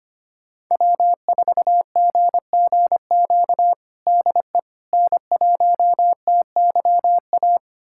Morse code, heard as a keyed tone, about 25 wpm, 700 Hz. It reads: W4GGQ DE N1TYA
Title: USA Callsign DE Callsign - 5 Characters - 25wpm